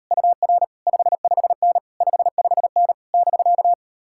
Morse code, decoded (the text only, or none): UR 55N 55N <BK>